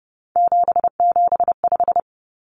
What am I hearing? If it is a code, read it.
Morse code, heard as a keyed tone, about 30 wpm, 700 Hz. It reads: 775
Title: Numbers - 3-Digit 30wpm